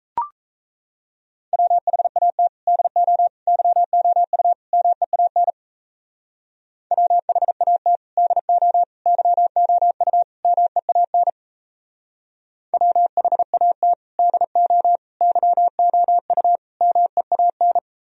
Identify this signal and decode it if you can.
Morse code, heard as a keyed tone, about 33 wpm, 700 Hz. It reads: WHAT DO YOU MEAN WHAT DO YOU MEAN WHAT DO YOU MEAN
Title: Speed-Racing - Sets of 4 Words 28wpm